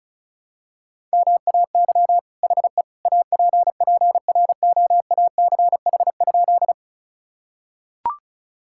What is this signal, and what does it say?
Morse code, heard as a keyed tone, about 36 wpm, 700 Hz. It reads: MAY HE APPROACH?
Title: Sentences from Top 1000 Words - Review Part 6 - 35wpm